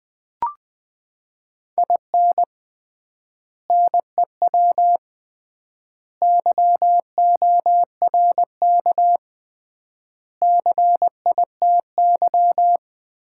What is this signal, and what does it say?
Morse code, heard as a keyed tone, about 21 wpm, 700 Hz. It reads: IN NEW YORK CITY
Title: 3x Word-Spacing - Sets of 4 Words - 20wpm